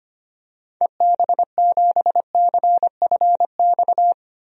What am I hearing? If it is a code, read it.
Morse code, heard as a keyed tone, about 25 wpm, 700 Hz. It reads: EB7CFX
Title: DXCC-100 Call Signs - 6 Characters - 25wpm